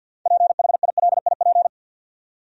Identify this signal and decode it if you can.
Morse code, heard as a keyed tone, about 52 wpm, 700 Hz. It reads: PHILIP
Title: USA Common Names - 50wpm